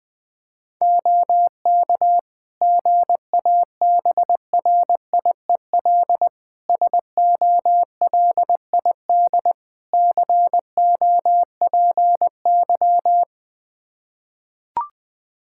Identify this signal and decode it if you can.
Morse code, heard as a keyed tone, about 20 wpm, 700 Hz. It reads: OK GABRIEL SOLID COPY
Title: Practice QSO - Exchange 3 - More Info - 20wpm